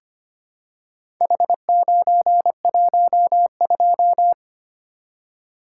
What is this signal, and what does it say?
Morse code, heard as a keyed tone, about 25 wpm, 700 Hz. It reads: H912
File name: practice / 25wpm / Four Letter-Number Combinations - 25wpm